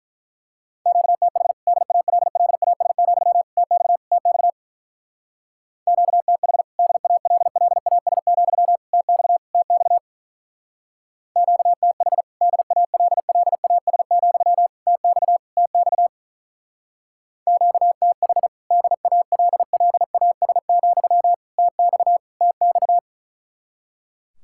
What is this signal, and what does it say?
Morse code, heard as a keyed tone, about 44 wpm, 700 Hz. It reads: QTH DALLAS, TX TX QTH DALLAS, TX TX QTH DALLAS, TX TX QTH DALLAS, TX TX
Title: Speed-Racing Common QSO Phrases Encore - 35wpm